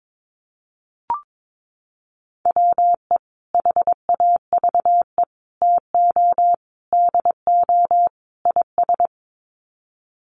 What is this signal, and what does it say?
Morse code, heard as a keyed tone, about 22 wpm, 700 Hz. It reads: WE HAVE TO DO IS
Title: Sets of 5 Words Encore 22wpm